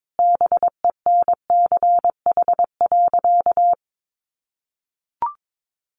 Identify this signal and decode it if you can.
Morse code, heard as a keyed tone, about 22 wpm, 700 Hz. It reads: BENCH.
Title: Sentences from Top 1400 Words 22wpm